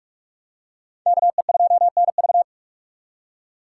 Morse code, heard as a keyed tone, about 41 wpm, 700 Hz. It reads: KE2NV